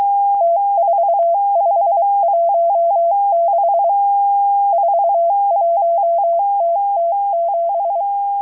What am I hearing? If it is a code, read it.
Morse code, heard as a keyed tone, about 23 wpm, 680 Hz. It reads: T4516 41TT7
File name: HM02- -0437utc-7351khz